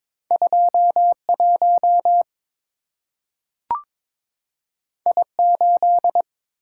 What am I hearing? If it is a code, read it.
Morse code, heard as a keyed tone, about 22 wpm, 700 Hz. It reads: 21 I8
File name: practice / 22wpm / Two Letter-Number Combinations - 22wpm